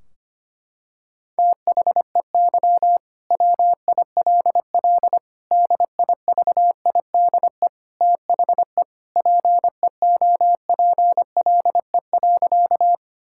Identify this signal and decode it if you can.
Morse code, heard as a keyed tone, about 25 wpm, 700 Hz. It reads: THEY WILL DIVIDE THE PEOPLE.
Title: Sentences from Top 1600 Words 25wpm